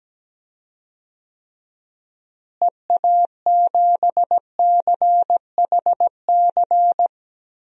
Morse code, heard as a keyed tone, about 17 wpm, 700 Hz. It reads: EA7CHC
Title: DXCC-100 Call Signs - 6 Characters - Repeated 3x - 17wpm